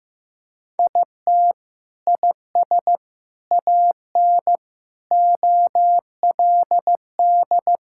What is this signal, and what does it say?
Morse code, heard as a keyed tone, about 15 wpm, 700 Hz. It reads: IT IS AN OLD
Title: Sentences from Top 1600 Words 15wpm